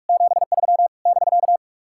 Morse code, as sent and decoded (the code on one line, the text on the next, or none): --... ...-- / -...-.-
73 <BK>